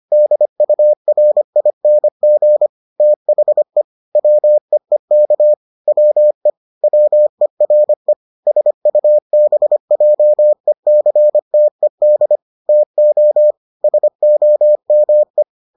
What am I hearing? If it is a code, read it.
Morse code, heard as a keyed tone, about 25 wpm, 600 Hz. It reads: DURING THE WEEK WE WERE SUBJECTED TO SOME